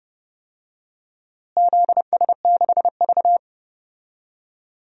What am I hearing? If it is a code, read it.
Morse code, heard as a keyed tone, about 29 wpm, 700 Hz. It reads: ZS6V
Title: DXCC-100 Call Signs - 4 Characters - Repeated 2x - 30wpm